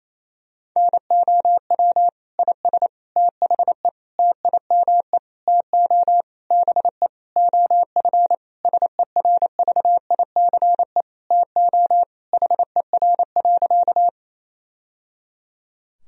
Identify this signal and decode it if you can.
Morse code, heard as a keyed tone, about 28 wpm, 700 Hz. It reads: NOW IS THE TIME TO BE OF SERVICE TO HER.
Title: Sentences from Top 1000 Words - Review Part 4 - 28wpm